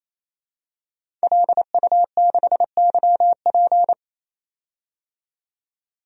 Morse code, heard as a keyed tone, about 27 wpm, 700 Hz. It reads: LU6YP
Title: DXCC-100 Call Signs - 5 Characters - 28wpm